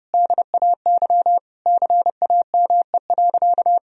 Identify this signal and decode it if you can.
Morse code, heard as a keyed tone, about 30 wpm, 700 Hz. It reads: DAY CAME.
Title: Sentences from Top 1000 Words - Review Part 2 - 30wpm